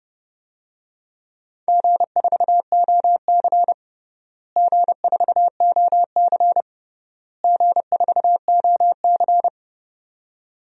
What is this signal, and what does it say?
Morse code, heard as a keyed tone, about 32 wpm, 700 Hz. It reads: G4OC G4OC G4OC
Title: DXCC-100 Call Signs - 4 Characters - Repeated 3x - 30wpm